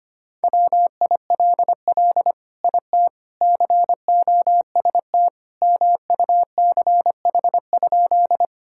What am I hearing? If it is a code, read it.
Morse code, heard as a keyed tone, about 25 wpm, 700 Hz. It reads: WILL IT COST MUCH?
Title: Sentences from Top 1000 Words - Review Part 5 - 25wpm